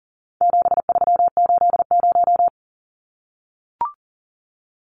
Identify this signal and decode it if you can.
Morse code, heard as a keyed tone, about 40 wpm, 700 Hz. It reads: 7380
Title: Numbers - 4-Digit 40wpm